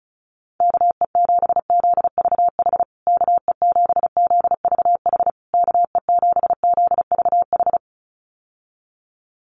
Morse code, heard as a keyed tone, about 34 wpm, 700 Hz. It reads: KE7ZVH KE7ZVH KE7ZVH
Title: USA Call Signs Repeated 3x 35wpm